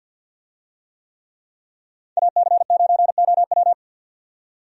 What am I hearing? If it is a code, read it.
Morse code, heard as a keyed tone, about 51 wpm, 700 Hz. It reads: AC9OW
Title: USA Call Signs - 5 Characters - 50wpm